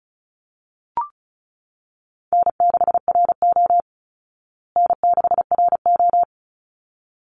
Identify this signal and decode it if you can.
Morse code, heard as a keyed tone, about 34 wpm, 700 Hz. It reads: N6RO N6RO
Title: Easy Call Signs Repeated 2x - 35wpm